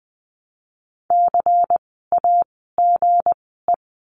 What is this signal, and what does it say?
Morse code, heard as a keyed tone, about 20 wpm, spaced as for 15 wpm, 700 Hz. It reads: CAGE
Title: 160 Lesson 26f - Letter g - Words 20wpm